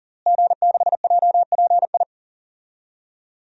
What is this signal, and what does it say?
Morse code, heard as a keyed tone, about 40 wpm, 700 Hz. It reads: G6JPI
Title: DXCC-100 Call Signs - 5 Characters - Repeated 3x - 40wpm